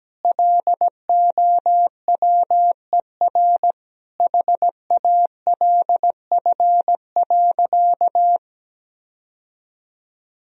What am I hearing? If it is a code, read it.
Morse code, heard as a keyed tone, about 17 wpm, 700 Hz. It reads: LOWER HALF.
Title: Sentences from Top 1000 Words - Review Part 4 - 17wpm